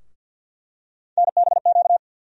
Morse code, heard as a keyed tone, about 51 wpm, 700 Hz. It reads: NBQ
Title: USA Call Sign Postfixes - 3 Letters 50wpm